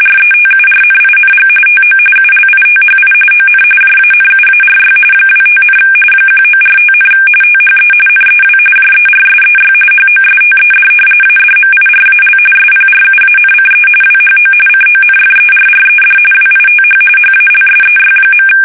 RTTY, about 45.45 baud. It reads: RETURNING TO ULITHI ON JANUARY 25, 1945, THE NEW JERSEY HOSTED ADMIRAL SPRUANCE WHO CONFERRED WITH HALSEY.
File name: fsk850 / battleshipnj